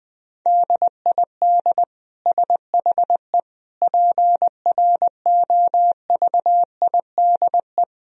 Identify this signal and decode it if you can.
Morse code, heard as a keyed tone, about 20 wpm, 700 Hz. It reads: DID SHE PROVIDE
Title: Sentences from Top 1000 Words - Review Part 4 - 20wpm